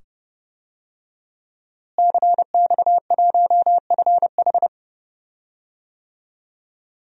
Morse code, heard as a keyed tone, about 30 wpm, 700 Hz. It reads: CX1FH
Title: DXCC-100 Call Signs - 5 Characters - Repeated 2x - 30wpm